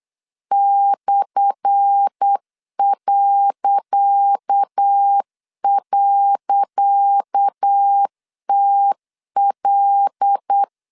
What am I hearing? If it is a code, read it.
Morse code, heard as a keyed tone, about 8 wpm, 800 Hz. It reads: /..TL
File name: LEZIONI / 10 SIGNS / 10C DOT 8wpm 3649